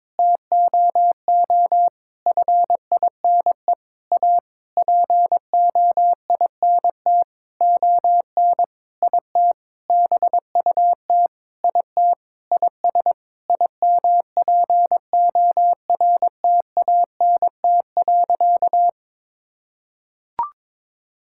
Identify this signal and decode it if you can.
Morse code, heard as a keyed tone, about 22 wpm, 700 Hz. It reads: TOO FINE A POINT ON IT BUT IT IS IMPORTANT.
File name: practice / 22wpm / Sentences with 8-Word Phrases - 22wpm